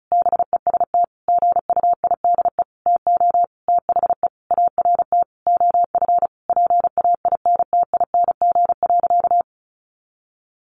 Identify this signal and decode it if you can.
Morse code, heard as a keyed tone, about 34 wpm, 700 Hz. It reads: BEST GUIDE TO THE ART OF PAINTING.
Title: Sentences from Top 1200 Words 35wpm